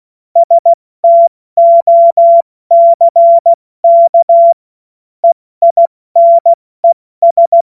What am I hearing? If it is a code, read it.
Morse code, heard as a keyed tone, about 16 wpm, spaced as for 14 wpm, 680 Hz. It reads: STOCK EINES
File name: Morsekurs / Witze / Witz 02 April 2024 70 F80